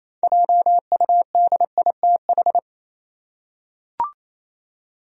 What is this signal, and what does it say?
Morse code, heard as a keyed tone, about 28 wpm, 700 Hz. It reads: JUDITH
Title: USA Common Names - 28wpm